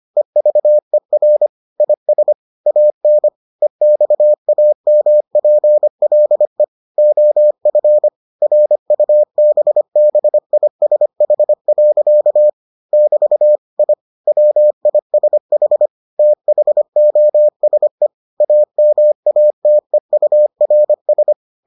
Morse code, heard as a keyed tone, about 25 wpm, 600 Hz. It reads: EVER IS AN EXAMPLE OF RUBBISH. <BT> I WISH THOSE AMATEURS